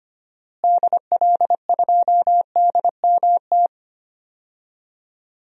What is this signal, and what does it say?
Morse code, heard as a keyed tone, about 25 wpm, 700 Hz. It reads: DL2DMT
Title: DXCC-100 Call Signs - 6 Characters - Repeated 2x - 25wpm